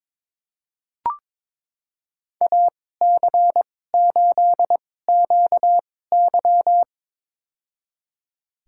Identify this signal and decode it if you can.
Morse code, heard as a keyed tone, about 22 wpm, spaced as for 16 wpm, 700 Hz. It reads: AC8QY
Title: USA Call Signs - 5 Characters - 2x Character Spacing - 22wpm